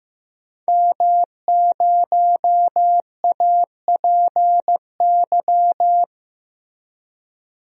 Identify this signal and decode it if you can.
Morse code, heard as a keyed tone, about 15 wpm, 700 Hz. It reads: M0APY
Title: DXCC-100 Call Signs - 5 Characters - Repeated 2x - 15wpm